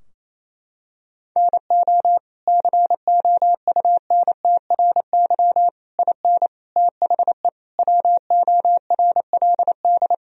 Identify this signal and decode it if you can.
Morse code, heard as a keyed tone, about 28 wpm, 700 Hz. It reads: NO COUNTRY IN THE WORLD
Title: Sentences from Top 1000 Words - Review Part 3 - 28wpm